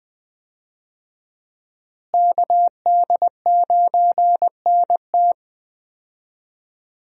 Morse code, heard as a keyed tone, about 20 wpm, 700 Hz. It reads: KD9NT
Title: USA Call Signs - 5 Characters - Repeated 2x - 20wpm